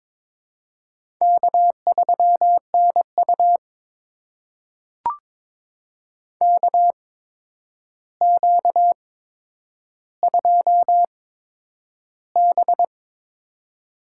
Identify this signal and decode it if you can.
Morse code, heard as a keyed tone, about 22 wpm, spaced as for 9 wpm, 700 Hz. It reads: K3NU KQ2B
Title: USA Call Signs - 4 Characters - 8x Character Spacing - 22wpm